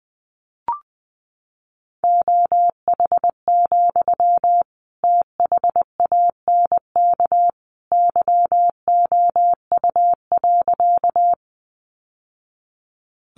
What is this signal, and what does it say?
Morse code, heard as a keyed tone, about 20 wpm, 700 Hz. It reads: OH, THANK YOU.
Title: Sentences from Top 600 Words 20wpm